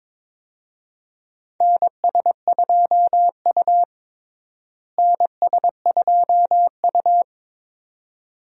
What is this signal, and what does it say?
Morse code, heard as a keyed tone, about 21 wpm, 700 Hz. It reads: NS2U NS2U
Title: USA Call Signs - 4 Characters - Repeated 2x - 22wpm